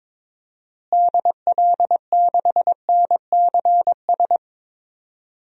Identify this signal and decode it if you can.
Morse code, heard as a keyed tone, about 22 wpm, 700 Hz. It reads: DL6NCS
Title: DXCC-100 Call Signs - 6 Characters - Repeated 2x - 22wpm